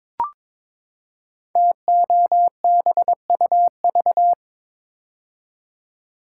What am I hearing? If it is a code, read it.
Morse code, heard as a keyed tone, about 22 wpm, 700 Hz. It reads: TOBUV
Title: Code Groups - 5 Letters - 22wpm